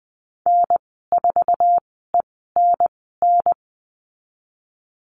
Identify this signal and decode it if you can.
Morse code, heard as a keyed tone, about 20 wpm, spaced as for 14 wpm, 700 Hz. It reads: N4ENN
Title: 069 Lesson 12g - Letter d - Call Signs 20wpm